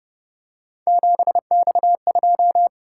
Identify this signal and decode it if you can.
Morse code, heard as a keyed tone, about 29 wpm, 700 Hz. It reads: 7X2
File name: Speed-Racing - Three Letter-Number Combinations - 25wpm